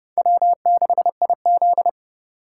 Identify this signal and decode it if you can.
Morse code, heard as a keyed tone, about 30 wpm, 700 Hz. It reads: W6IZ
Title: USA Call Signs - 4 Characters - 3x Character Spacing - 30wpm